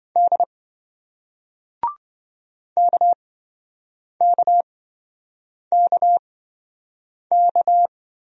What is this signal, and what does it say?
Morse code, heard as a keyed tone, about 27 wpm, spaced as for 7 wpm, 700 Hz. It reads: D KKKK